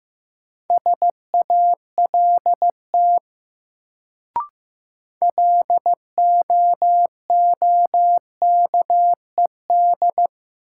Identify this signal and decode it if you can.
Morse code, heard as a keyed tone, about 15 wpm, 700 Hz. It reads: SALT LOOKED
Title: Sentences from Top 800 Words - New Word Drill - 15wpm